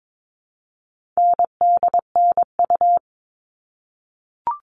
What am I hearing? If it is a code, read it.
Morse code, heard as a keyed tone, about 22 wpm, 700 Hz. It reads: NDNU
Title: All Cut Numbers - 4-Digit 22wpm